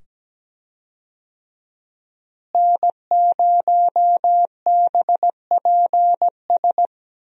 Morse code, heard as a keyed tone, about 17 wpm, 700 Hz. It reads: N0BPS
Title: USA Call Signs - 5 Characters - Repeated 2x - 17wpm